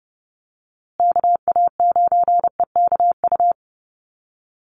Morse code, heard as a keyed tone, about 30 wpm, 700 Hz. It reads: KA9EKU
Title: USA Call Signs Repeated 2x 30wpm